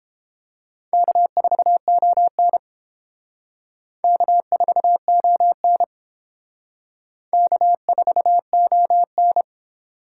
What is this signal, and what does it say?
Morse code, heard as a keyed tone, about 29 wpm, 700 Hz. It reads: K4ON K4ON K4ON